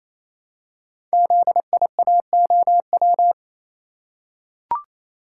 Morse code, heard as a keyed tone, about 29 wpm, 700 Hz. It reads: ZIAOW
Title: Code Groups - 5 Letter:Numbers - 28wpm